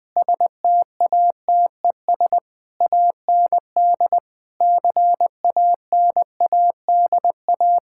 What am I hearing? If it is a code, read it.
Morse code, heard as a keyed tone, about 20 wpm, 700 Hz. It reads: STATES AND CANADA
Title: Sentences with 5-Word Phrases - 20wpm